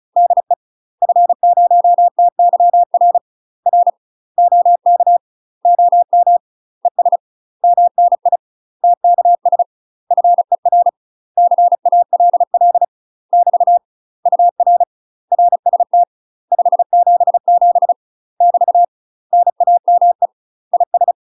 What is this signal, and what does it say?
Morse code, heard as a keyed tone, about 36 wpm, 700 Hz. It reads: DE F0TYR R OK OM ES MNI TKS FER CALL <BT> UR RST 577 <BT> NAME IS
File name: UK MASTER QSOs TRAINING / 35WPM / UK MASTER QSOs 0026